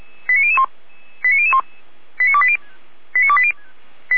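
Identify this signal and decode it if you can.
Five-tone selective call (ZVEI), heard as quick runs of five tones, tones of about 70 ms. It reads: 89001 89001 89280
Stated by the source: ZVEI five-tone call received by SDR.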